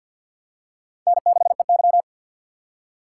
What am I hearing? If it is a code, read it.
Morse code, heard as a keyed tone, about 52 wpm, 700 Hz. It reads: N6EY
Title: USA Call Signs - 6x Character Spacing - 50wpm